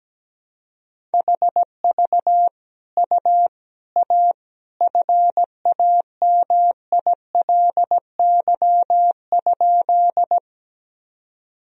Morse code, heard as a keyed tone, about 17 wpm, 700 Hz. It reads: HV U A FAMILY?
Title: Sentences from Top 400 Words and Abbreviations 17wpm